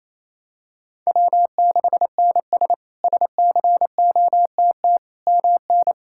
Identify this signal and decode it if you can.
Morse code, heard as a keyed tone, about 28 wpm, 700 Hz. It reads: W6NS SCOTT MN
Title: K1USN SST QSO Practice - 28wpm